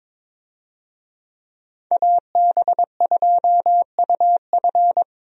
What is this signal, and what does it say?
Morse code, heard as a keyed tone, about 21 wpm, 700 Hz. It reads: AB2UF
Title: USA Call Signs - 5 Characters - Repeated 2x - 22wpm